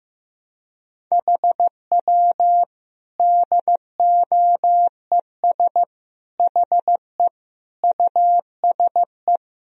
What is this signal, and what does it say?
Morse code, heard as a keyed tone, about 15 wpm, 700 Hz. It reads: HW DOES HE USE
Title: Sentences from Top 200 Words and Abbreviations 15wpm